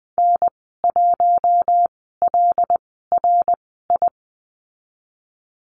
Morse code, heard as a keyed tone, about 20 wpm, spaced as for 15 wpm, 700 Hz. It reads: N1LRI
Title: 076 Lesson 13g - Letter L - Call Signs 20wpm